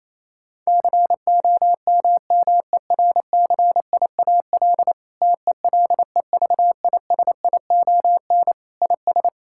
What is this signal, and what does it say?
Morse code, heard as a keyed tone, about 28 wpm, 700 Hz. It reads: COMMERCIAL TELEVISION IS
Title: Sentences from Top 1000 Words - Review Part 6 - 28wpm